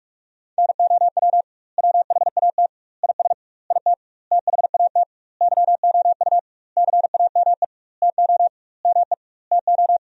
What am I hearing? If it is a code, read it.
Morse code, heard as a keyed tone, about 46 wpm, 700 Hz. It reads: NOW WHAT IS IT THAT YOU CAME TO ME TO
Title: 2x Word-Spacing - Sentences from Top 200 Words - 45wpm